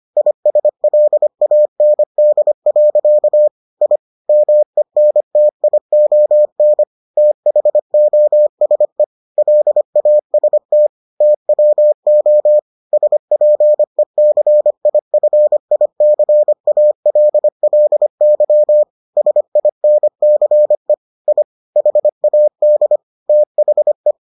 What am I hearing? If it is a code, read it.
Morse code, heard as a keyed tone, about 25 wpm, 600 Hz. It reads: ISLAND. I MENTION THOSE LAST TWO SPECIFICALLY SINCE I HAD THE